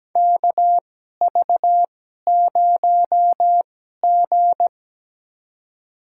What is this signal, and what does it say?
Morse code, heard as a keyed tone, about 17 wpm, spaced as for 12 wpm, 700 Hz. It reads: KV0G
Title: USA Call Signs - 4 Characters - 2x Character Spacing - 17wpm